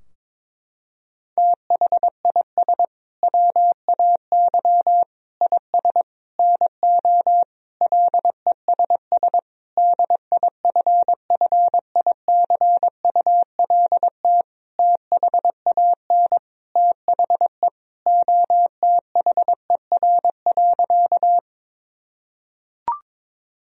Morse code, heard as a keyed tone, about 22 wpm, 700 Hz. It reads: THIS WAY IS NO LESS DIFFICULT THAN THE OTHER.
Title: Sentences from Top 1000 Words - Review Part 2 - 22wpm